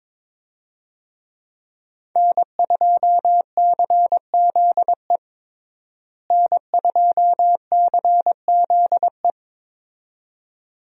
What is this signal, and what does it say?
Morse code, heard as a keyed tone, about 22 wpm, 700 Hz. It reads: N2CZE N2CZE
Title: USA Call Signs - 5 Characters - Repeated 2x - 22wpm